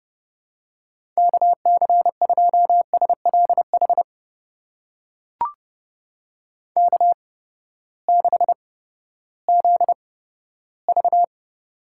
Morse code, heard as a keyed tone, about 31 wpm, spaced as for 15 wpm, 700 Hz. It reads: KC2SLH K6ZV
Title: USA Call Signs - 8x Character Spacing - 30wpm